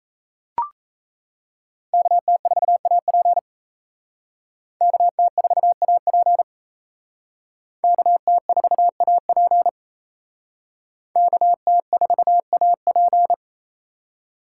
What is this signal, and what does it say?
Morse code, heard as a keyed tone, about 35 wpm, 700 Hz. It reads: KT4AP KT4AP KT4AP KT4AP
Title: Speed Racing - USA 2x2 Call Signs 28wpm